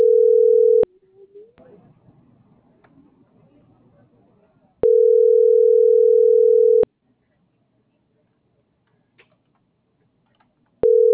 Ringback tone (North America), 440 + 480 Hz.